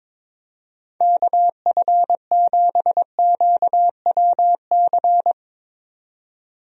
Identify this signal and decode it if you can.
Morse code, heard as a keyed tone, about 22 wpm, 700 Hz. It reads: KF7QWC
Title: USA Call Signs - 6 Characters - 22wpm